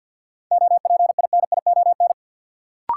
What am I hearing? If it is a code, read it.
Morse code, heard as a keyed tone, about 51 wpm, 700 Hz. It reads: OPINION
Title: Sentences from Top 1200 Words - New Word Drill - 50wpm